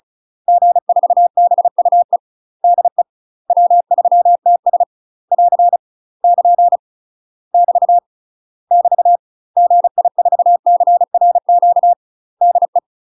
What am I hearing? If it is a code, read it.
Morse code, heard as a keyed tone, about 34 wpm, 700 Hz. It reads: G4BUE DE W3TS <AR> <KN> <BT> <BT> GI4CRQ DE
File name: UK MASTER QSOs TRAINING / 35WPM / UK MASTER QSOs 0018